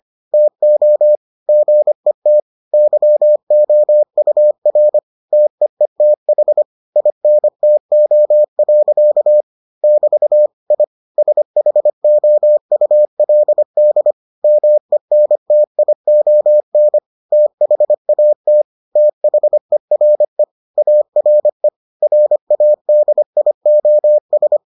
Morse code, heard as a keyed tone, about 25 wpm, 600 Hz. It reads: TO GET YOUR TEETH INTO. <BT> I SHOULD MENTION THAT THERE ARE RADIOS